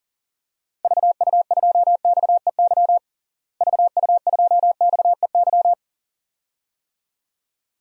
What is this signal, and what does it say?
Morse code, heard as a keyed tone, about 40 wpm, 700 Hz. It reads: VU2XEY VU2XEY
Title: DXCC-100 Call Signs - 6 Characters - Repeated 2x - 40wpm